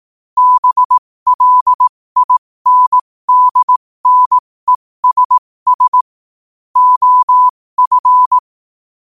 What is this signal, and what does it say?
Morse code, heard as a keyed tone, about 18 wpm, spaced as for 15 wpm, 1000 Hz. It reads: BLINDNESS OF
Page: 15 WPM morse code quotes